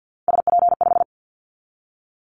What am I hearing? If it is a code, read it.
Morse code, heard as a keyed tone, about 50 wpm, 700 Hz. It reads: SP5
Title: Callsign Prefixes for Top 25 DXCC - 50wpm